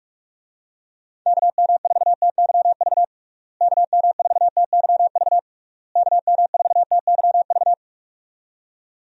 Morse code, heard as a keyed tone, about 46 wpm, 700 Hz. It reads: KM4TYV KM4TYV KM4TYV
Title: USA Call Signs - 6 Characters - Repeated 3x - 45wpm